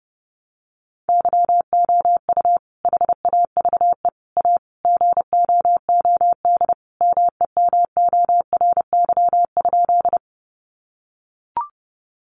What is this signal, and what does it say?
Morse code, heard as a keyed tone, about 30 wpm, 700 Hz. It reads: YOU HAVE A GOOD MEMORY?
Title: Sentences from Top 1300 Words 30wpm